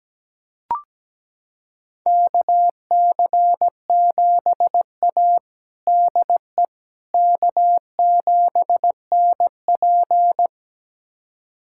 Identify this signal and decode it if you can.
Morse code, heard as a keyed tone, about 17 wpm, 700 Hz. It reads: KC7A DE K7NP
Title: USA Callsign DE Callsign - 4 Characters - 17wpm